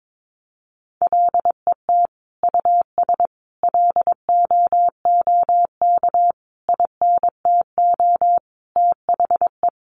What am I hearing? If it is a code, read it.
Morse code, heard as a keyed tone, about 22 wpm, 700 Hz. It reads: LET US LOOK INTO THE
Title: Sentences from Top 1400 Words 22wpm